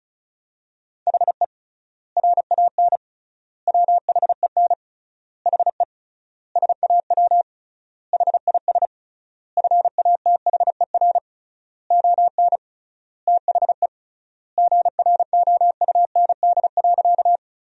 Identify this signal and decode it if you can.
Morse code, heard as a keyed tone, about 36 wpm, 700 Hz. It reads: HE RAN WHEN HE SAW HIS FATHER ON THE GROUND.
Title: 3x Word-Spacing - Sentences from Top 500 Words - 35wpm